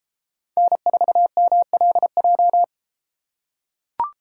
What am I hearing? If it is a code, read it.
Morse code, heard as a keyed tone, about 31 wpm, 700 Hz. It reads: N4MLJ